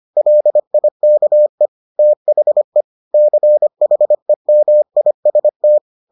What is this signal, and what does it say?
Morse code, heard as a keyed tone, about 25 wpm, 600 Hz. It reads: LIKE THE CHEMIST